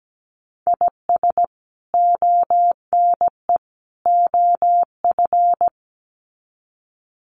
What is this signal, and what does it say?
Morse code, heard as a keyed tone, about 17 wpm, 700 Hz. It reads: IS ONE OF
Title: Sets of 3 Words 17wpm